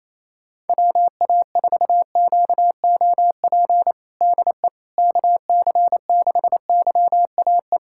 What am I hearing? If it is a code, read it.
Morse code, heard as a keyed tone, about 28 wpm, 700 Hz. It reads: WA4QOP DE KC6YAE
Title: USA Callsign DE Callsign 28wpm